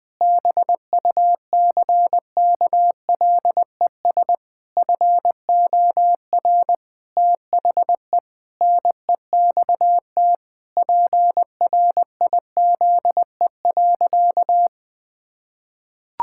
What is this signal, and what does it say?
Morse code, heard as a keyed tone, about 20 wpm, 700 Hz. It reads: BUCKLES FOR THE NEXT PRIZE.
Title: Pangram Challenge v2 - 20wpm